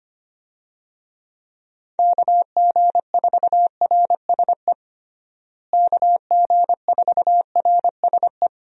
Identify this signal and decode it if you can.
Morse code, heard as a keyed tone, about 25 wpm, 700 Hz. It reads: KG4RSE KG4RSE
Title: USA Call Signs - 6 Characters - Repeated 2x - 25wpm